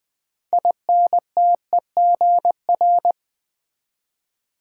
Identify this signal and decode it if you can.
Morse code, heard as a keyed tone, about 20 wpm, 700 Hz. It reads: INTEGR
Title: Build A Word - 3 Letters At A Time - 8-Letter Words - 20wpm